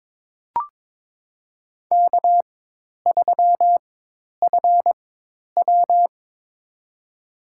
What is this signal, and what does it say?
Morse code, heard as a keyed tone, about 23 wpm, spaced as for 10 wpm, 700 Hz. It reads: K3FW
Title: USA Call Signs - 4 Characters - 4x Character Spacing - 22wpm